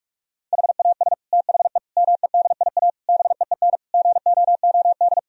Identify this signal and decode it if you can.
Morse code, heard as a keyed tone, about 46 wpm, 700 Hz. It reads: HAS THE MEDIA BEEN GOOD
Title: Sentences from Top 1000 Words - Review Part 4 - 45wpm